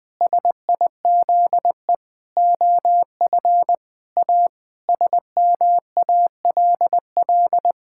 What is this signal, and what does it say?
Morse code, heard as a keyed tone, about 21 wpm, 700 Hz. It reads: SIZE OF A SMALL
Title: Sentences with 4-Word Phrases - 20wpm